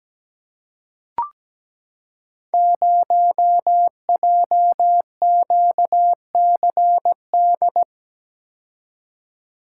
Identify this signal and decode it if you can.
Morse code, heard as a keyed tone, about 17 wpm, 700 Hz. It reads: 0JQCD
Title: Code Groups - 5 Letter:Numbers - 17wpm